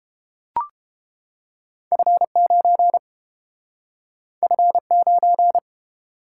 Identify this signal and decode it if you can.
Morse code, heard as a keyed tone, about 30 wpm, 700 Hz. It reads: F9 F9